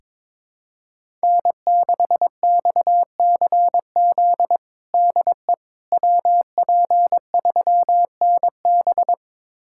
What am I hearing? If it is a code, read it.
Morse code, heard as a keyed tone, about 22 wpm, 700 Hz. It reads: N6XCZ DE WP3NB
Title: USA Callsign DE Callsign - 5 Characters - 22wpm